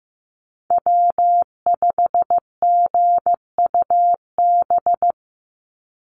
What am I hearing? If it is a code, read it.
Morse code, heard as a keyed tone, about 15 wpm, 700 Hz. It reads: W5GUB
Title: USA Call Signs 15wpm